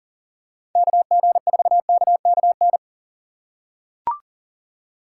Morse code, heard as a keyed tone, about 40 wpm, 700 Hz. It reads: KG4KKN
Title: USA Call Signs - 6 Characters - 6x Character Spacing - 40wpm